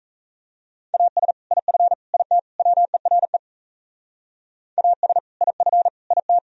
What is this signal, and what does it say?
Morse code, heard as a keyed tone, about 34 wpm, 700 Hz. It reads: AS IF IT WERE AS IF IT
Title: Speed-Racing - Sets of 4 Words 28wpm